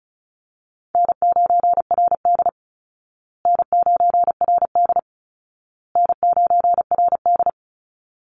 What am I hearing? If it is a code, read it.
Morse code, heard as a keyed tone, about 34 wpm, 700 Hz. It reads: N9RD N9RD N9RD
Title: Call Signs Repeated 2x - 35wpm